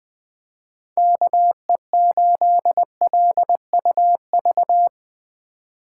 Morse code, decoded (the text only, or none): KE8LUV